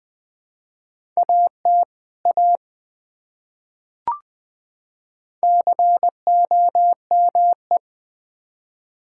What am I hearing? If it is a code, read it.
Morse code, heard as a keyed tone, about 20 wpm, 700 Hz. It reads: AT A COME
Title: Build A Phrase - 2-Word Phrases - 20wpm